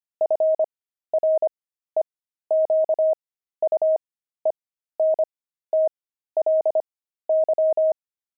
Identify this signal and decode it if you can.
Morse code, heard as a keyed tone, about 25 wpm, spaced as for 13 wpm, 630 Hz. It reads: FREQUENTLY
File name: pangram-fs-13-cs-25-linoleum